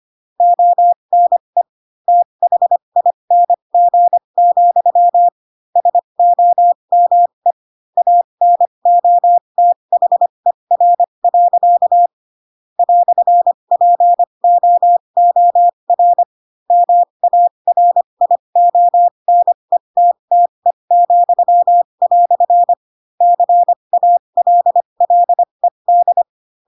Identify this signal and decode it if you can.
Morse code, heard as a keyed tone, about 25 wpm, spaced as for 22 wpm, 700 Hz. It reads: ONE THING, SOME ANOTHER. "POOR MARIONETTE," CALLED
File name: Pinocchio / Pinocchio ch3 18 25 ik1hge